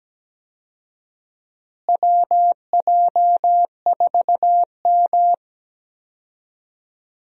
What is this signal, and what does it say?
Morse code, heard as a keyed tone, about 17 wpm, 700 Hz. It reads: WJ4M
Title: USA Call Signs - 4 Characters - Repeated 3x - 17wpm